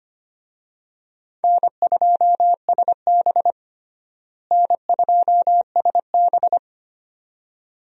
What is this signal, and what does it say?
Morse code, heard as a keyed tone, about 25 wpm, 700 Hz. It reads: N2SB N2SB
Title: USA Call Signs - 4 Characters - Repeated 2x - 25wpm